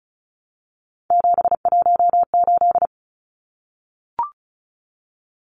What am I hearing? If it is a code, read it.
Morse code, heard as a keyed tone, about 34 wpm, 700 Hz. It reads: 718
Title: Numbers - 3-Digit 35wpm